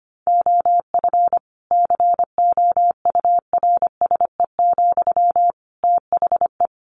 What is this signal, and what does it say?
Morse code, heard as a keyed tone, about 25 wpm, 700 Hz. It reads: OF COURSE, THE
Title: Sentences from Top 1500 Words 25wpm